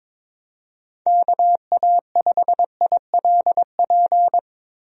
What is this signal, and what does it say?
Morse code, heard as a keyed tone, about 22 wpm, 700 Hz. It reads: KA5ILP
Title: USA Call Signs - 6 Characters - Repeated 2x - 22wpm